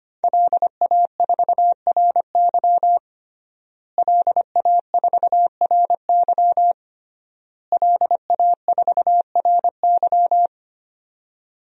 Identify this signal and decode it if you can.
Morse code, heard as a keyed tone, about 25 wpm, 700 Hz. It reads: LA4RY LA4RY LA4RY
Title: DXCC-100 Call Signs - 5 Characters - Repeated 3x - 25wpm